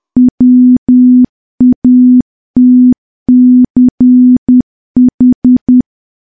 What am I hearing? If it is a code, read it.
Morse code, heard as a keyed tone, about 10 wpm, 250 Hz. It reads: WATCH
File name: morse / morse-training-10WPM